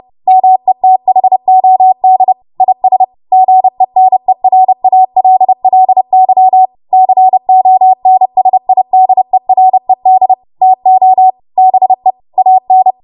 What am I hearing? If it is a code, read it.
Morse code, heard as a keyed tone, about 30 wpm, 750 Hz. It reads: METHOD IS GENERALLY CONSIDERED TO BE AN
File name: Morse / 30WPM